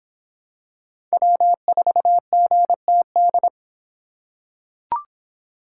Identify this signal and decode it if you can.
Morse code, heard as a keyed tone, about 23 wpm, 700 Hz. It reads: W4GTD